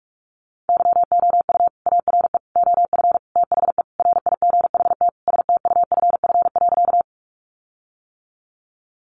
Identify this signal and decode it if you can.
Morse code, heard as a keyed tone, about 44 wpm, 700 Hz. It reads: YOU ARE OF THE RIGHT STUFF.
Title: Sentences from Top 1400 Words 45wpm